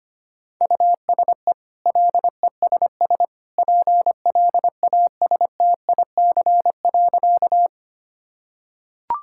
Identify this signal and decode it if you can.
Morse code, heard as a keyed tone, about 25 wpm, 700 Hz. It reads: USE LESS PLASTIC.
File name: Easy Sentences with Hints - 25wpm